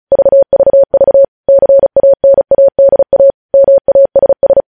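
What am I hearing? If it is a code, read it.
Morse code, heard as a keyed tone, about 36 wpm, 550 Hz. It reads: VVV CANADA MASS
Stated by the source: Morse code news headlines at 35.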